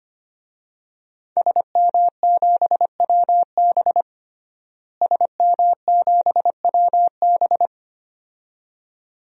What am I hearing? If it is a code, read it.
Morse code, heard as a keyed tone, about 25 wpm, 700 Hz. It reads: SM7WB SM7WB
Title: DXCC-100 Call Signs - 5 Characters - Repeated 2x - 25wpm